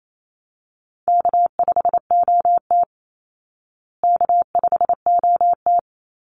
Morse code, heard as a keyed tone, about 28 wpm, 700 Hz. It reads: K5OT K5OT
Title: Easy Call Signs Repeated 2x - 28wpm